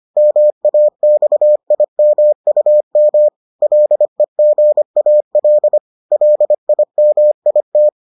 Morse code, heard as a keyed tone, about 25 wpm, 600 Hz. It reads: MAXIMUM LEGAL LIMIT